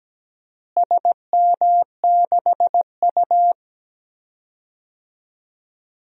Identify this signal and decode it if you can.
Morse code, heard as a keyed tone, about 17 wpm, 700 Hz. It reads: SM6U
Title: DXCC-100 Call Signs - 4 Characters - Repeated 2x - 17wpm